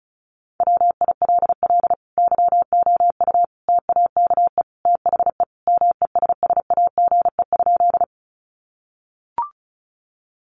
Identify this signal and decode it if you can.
Morse code, heard as a keyed tone, about 34 wpm, 700 Hz. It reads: WILL YOU TAKE THE MESSAGE?
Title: Sentences from Top 1100 Words 35wpm